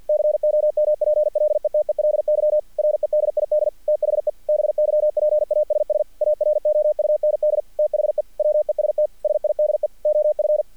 Morse code, heard as a keyed tone, about 48 wpm, 600 Hz. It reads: COMPLETELY REDID THE BYPASS AROUND THE WEST SIDE OF
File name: archived / CRSnet